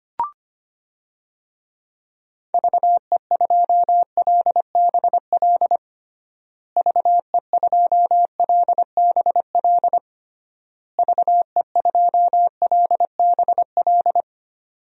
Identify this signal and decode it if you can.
Morse code, heard as a keyed tone, about 25 wpm, 700 Hz. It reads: VE2LBL VE2LBL VE2LBL
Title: DXCC-100 Call Signs - 6 Characters - Repeated 3x - 25wpm